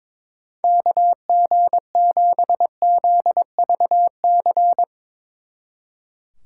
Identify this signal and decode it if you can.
Morse code, heard as a keyed tone, about 24 wpm, 700 Hz. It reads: KG7ZVC
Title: Speed Racing - USA 2x3 Call Signs 22wpm